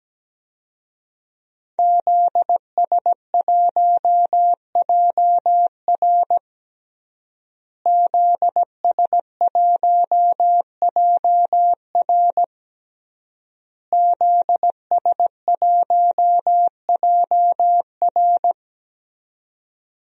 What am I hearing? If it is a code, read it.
Morse code, heard as a keyed tone, about 17 wpm, 700 Hz. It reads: ZS1JR ZS1JR ZS1JR
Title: DXCC-100 Call Signs - 5 Characters - Repeated 3x - 17wpm